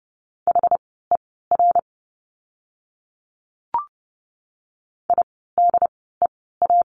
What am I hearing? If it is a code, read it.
Morse code, heard as a keyed tone, about 30 wpm, spaced as for 17 wpm, 700 Hz. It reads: HER IDEA
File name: course / 30:10wpm / 108 Lesson 18f - Letter m - Words 30wpm